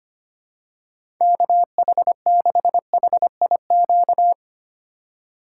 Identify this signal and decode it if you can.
Morse code, heard as a keyed tone, about 25 wpm, 700 Hz. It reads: KH6HIQ
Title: USA Call Signs - 6 Characters - 3x Character Spacing - 25wpm